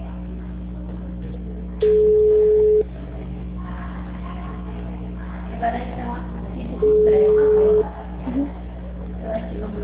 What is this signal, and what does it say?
Ringback tone (Europe, 425 Hz), with speech.